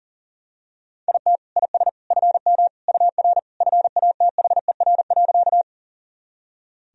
Morse code, heard as a keyed tone, about 40 wpm, 700 Hz. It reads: IT IS FM UR FATHER.
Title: Sentences from Top 300 Words and Abbreviations 40wpm